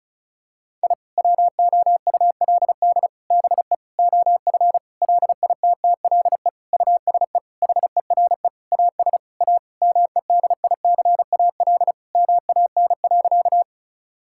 Morse code, heard as a keyed tone, about 36 wpm, 700 Hz. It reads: I WOULD BE OF LITTLE USE HERE AS A MEDICAL MAN.
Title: Sentences from Top 1000 Words - Review Part 5 - 35wpm